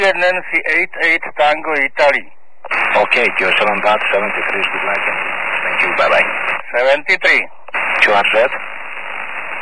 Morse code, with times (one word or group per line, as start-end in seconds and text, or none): none